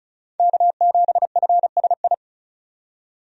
Morse code, heard as a keyed tone, about 36 wpm, 700 Hz. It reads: K7FSI
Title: USA Call Signs - 5 Characters - 35wpm